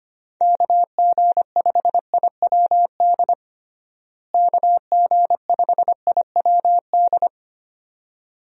Morse code, heard as a keyed tone, about 25 wpm, 700 Hz. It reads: KG5IWD KG5IWD
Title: USA Call Signs - 6 Characters - Repeated 2x - 25wpm